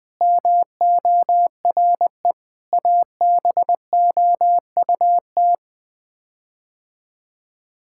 Morse code, heard as a keyed tone, about 20 wpm, 700 Hz. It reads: MORE ABOUT
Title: Sets of 8 Words - 20wpm